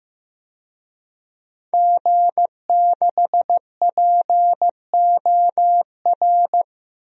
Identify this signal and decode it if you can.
Morse code, heard as a keyed tone, about 15 wpm, 700 Hz. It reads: G6POR
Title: DXCC-100 Call Signs - 5 Characters - 15wpm